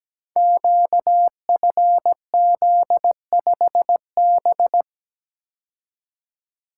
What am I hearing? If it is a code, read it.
Morse code, heard as a keyed tone, about 17 wpm, 700 Hz. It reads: QFZ5B
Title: Code Groups - 5 Letter:Numbers - 17wpm